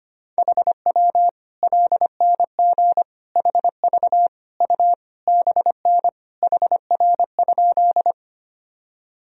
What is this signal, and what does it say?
Morse code, heard as a keyed tone, about 25 wpm, 700 Hz. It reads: HW LNG HV U BN HR?
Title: Sentences from Top 200 Words and Abbreviations 25wpm